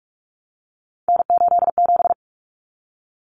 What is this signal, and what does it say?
Morse code, heard as a keyed tone, about 44 wpm, 700 Hz. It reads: N87
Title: Common Cut Numbers - 3-Digit 45wpm